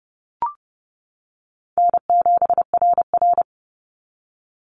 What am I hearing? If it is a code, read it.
Morse code, heard as a keyed tone, about 30 wpm, 700 Hz. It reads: N7RR
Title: NA SOTA Activator Callsigns 30wpm